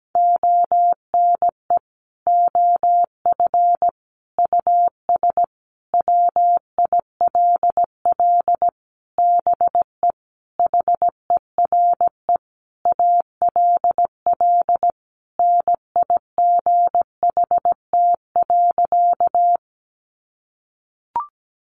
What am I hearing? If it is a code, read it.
Morse code, heard as a keyed tone, about 17 wpm, 700 Hz. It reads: ONE OF US WILL BE HERE ALL NIGHT.
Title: Sentences from Top 300 Words 17wpm